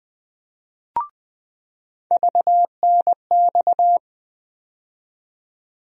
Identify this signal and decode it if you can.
Morse code, heard as a keyed tone, about 20 wpm, 700 Hz. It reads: VNX
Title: USA Call Sign Postfixes - 3 Letters 20wpm